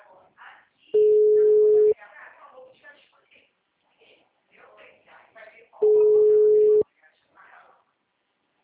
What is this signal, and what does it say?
Ringback tone (Europe, 425 Hz), with speech.
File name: out- -2013- - - .41548